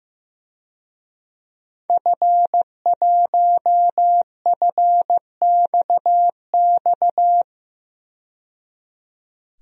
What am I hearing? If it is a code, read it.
Morse code, heard as a keyed tone, about 16 wpm, 700 Hz. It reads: F1FXX
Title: DXCC-100 Call Signs - 5 Characters - 15wpm